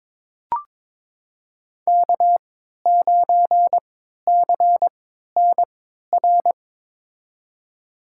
Morse code, heard as a keyed tone, about 22 wpm, spaced as for 12 wpm, 700 Hz. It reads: K9CNR
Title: USA Call Signs - 5 Characters - 3x Character Spacing - 22wpm